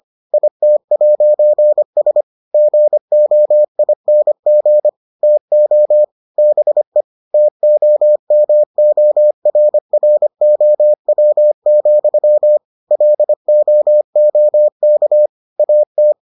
Morse code, heard as a keyed tone, about 25 wpm, 600 Hz. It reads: IT'S GOING TO BE TOMORROW, LOOK AT